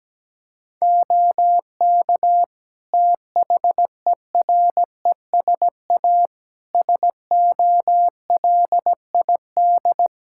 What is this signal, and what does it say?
Morse code, heard as a keyed tone, about 17 wpm, 700 Hz. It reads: OK THERESA SOLID
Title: Practice QSO - Exchange 3 - More Info - 17wpm